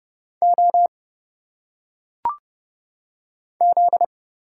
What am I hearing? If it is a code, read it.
Morse code, heard as a keyed tone, about 30 wpm, 700 Hz. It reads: O Z
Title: Single Letters - Phonetic Pronunciation 30wpm